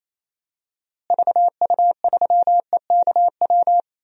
Morse code, heard as a keyed tone, about 28 wpm, 700 Hz. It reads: VU3EKW
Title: DXCC-100 Call Signs - 6 Characters - Repeated 3x - 28wpm